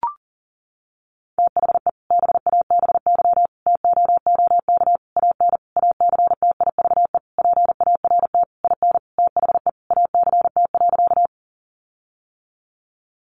Morse code, heard as a keyed tone, about 40 wpm, 700 Hz. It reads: THE BABY TOOK AN ACTIVE PART IN THE ACT.
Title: Sentences from Top 1300 Words 40wpm